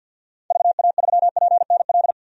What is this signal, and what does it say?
Morse code, heard as a keyed tone, about 52 wpm, 700 Hz. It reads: VA3PNL
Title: DXCC-100 Call Signs - 6 Characters - 50wpm